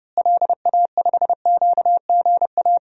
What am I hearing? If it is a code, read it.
Morse code, heard as a keyed tone, about 32 wpm, 700 Hz. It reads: LA5QGA
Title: DXCC-100 Call Signs - 6 Characters - Repeated 2x - 30wpm